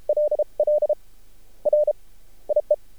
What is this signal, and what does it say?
Morse code, heard as a keyed tone, about 33 wpm, spaced as for 20 wpm, 600 Hz. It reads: LL RIE
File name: archived / CRSnet